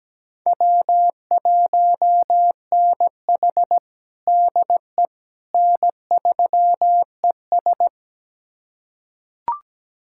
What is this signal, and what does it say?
Morse code, heard as a keyed tone, about 17 wpm, 700 Hz. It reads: W1NH DE N3ES
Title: USA Callsign DE Callsign - 4 Characters - 17wpm